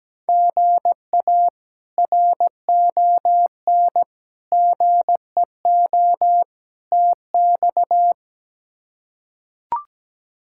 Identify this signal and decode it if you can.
Morse code, heard as a keyed tone, about 17 wpm, 700 Hz. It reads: GA RON GEO TX
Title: K1USN SST QSO Practice - 17wpm